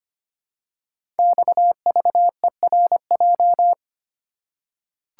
Morse code, heard as a keyed tone, about 25 wpm, 700 Hz. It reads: XVERJ
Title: Code Groups - 5 Letters - 25wpm